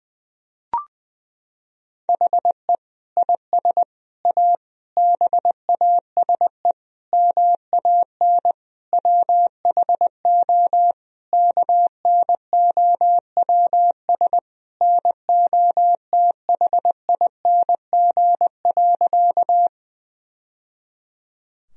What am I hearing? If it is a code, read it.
Morse code, heard as a keyed tone, about 21 wpm, 700 Hz. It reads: HE IS A BASE MAN WHO KNOWS NOTHING.
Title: Sentences from Top 1000 Words - Review Part 1 - 20wpm